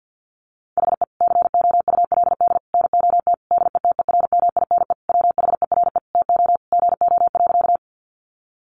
Morse code, heard as a keyed tone, about 50 wpm, 700 Hz. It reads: HE COULD NOT DETERMINE WHERE TO GO.
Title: Sentences from Top 700 Words 50wpm